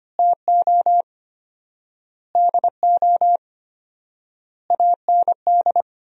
Morse code, heard as a keyed tone, about 25 wpm, 700 Hz. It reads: TO DO AND
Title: 4x Word-Spacing - Sentences from Top 500 Words - 25wpm